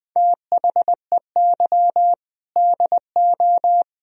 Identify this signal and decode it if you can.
Morse code, heard as a keyed tone, about 23 wpm, 700 Hz. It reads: THEY DO
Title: Speed-Racing - Sets of 3 Words 20wpm